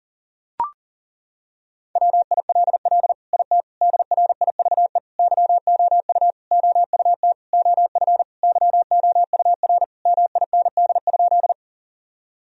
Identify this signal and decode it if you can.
Morse code, heard as a keyed tone, about 40 wpm, 700 Hz. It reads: WILL IT DRIVE YOU OUT OF YOUR MIND?
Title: Sentences from Top 1000 Words - Review Part 2 - 40wpm